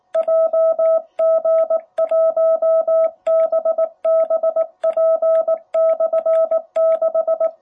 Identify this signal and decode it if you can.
Morse code, heard as a keyed tone, about 19 wpm, 660 Hz. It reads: JG1BBP/6
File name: jh8cbh / 2021cw